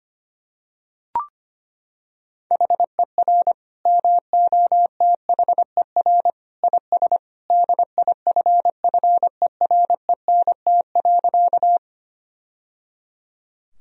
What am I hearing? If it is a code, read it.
Morse code, heard as a keyed tone, about 25 wpm, 700 Hz. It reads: HER MOTHER IS DIFFERENT.
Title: Sentences from Top 1000 Words - Review Part 3 - 25wpm